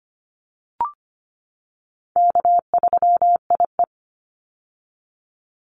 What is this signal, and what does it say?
Morse code, heard as a keyed tone, about 25 wpm, 700 Hz. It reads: K3IE
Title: Call Signs - 25wpm